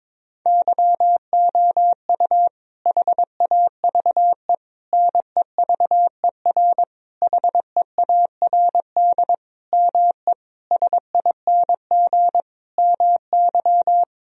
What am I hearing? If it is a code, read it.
Morse code, heard as a keyed tone, about 22 wpm, 700 Hz. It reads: YOU HAVE NEVER HEARD ME SING MY
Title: Sentences from Top 1000 Words - Review Part 2 - 22wpm